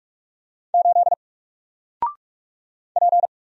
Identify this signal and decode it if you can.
Morse code, heard as a keyed tone, about 45 wpm, 700 Hz. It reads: 8 P
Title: Single Letter-Number 45wpm